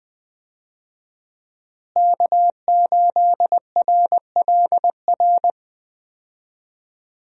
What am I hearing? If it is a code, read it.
Morse code, heard as a keyed tone, about 20 wpm, 700 Hz. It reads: K8RLR
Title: USA Call Signs - 5 Characters - Repeated 3x - 20wpm